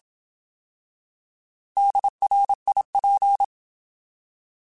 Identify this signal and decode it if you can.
Morse code, heard as a keyed tone, about 27 wpm, 800 Hz. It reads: DRIP